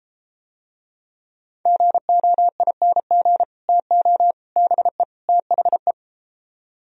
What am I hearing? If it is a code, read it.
Morse code, heard as a keyed tone, about 34 wpm, 700 Hz. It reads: GOING TO BE THE